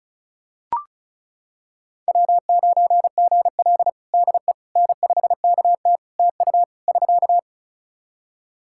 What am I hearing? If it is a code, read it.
Morse code, heard as a keyed tone, about 34 wpm, 700 Hz. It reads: W9GL DE N5KT TU <SK>
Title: Practice QSO - Exchange 4 - Ending - 35wpm